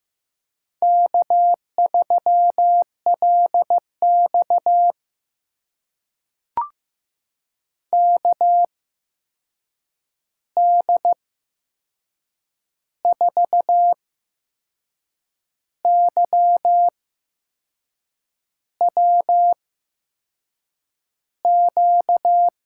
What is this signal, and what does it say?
Morse code, heard as a keyed tone, about 15 wpm, spaced as for 6 wpm, 700 Hz. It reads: K3LX KD4YWQ
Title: USA Call Signs - 8x Character Spacing - 15wpm